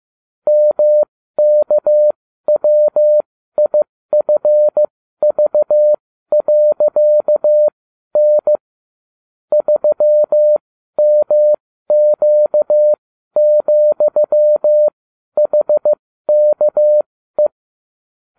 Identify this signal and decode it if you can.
Morse code, heard as a keyed tone, about 15 wpm, spaced as for 12 wpm, 600 Hz. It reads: MKWIFV.N 3MQ,HKE
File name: cw / ubacw29 2